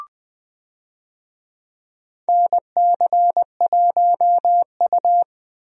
Morse code, heard as a keyed tone, about 20 wpm, 700 Hz. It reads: NC1U
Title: USA Call Signs - 4 Characters - 20wpm